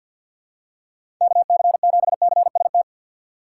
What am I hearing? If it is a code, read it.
Morse code, heard as a keyed tone, about 53 wpm, 700 Hz. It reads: KC7CST
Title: USA Call Signs - 6 Characters - 3x Character Spacing - 50wpm